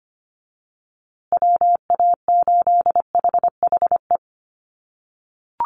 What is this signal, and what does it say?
Morse code, heard as a keyed tone, about 25 wpm, 700 Hz. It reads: WA8HHE
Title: USA Call Signs Repeated 2x 25wpm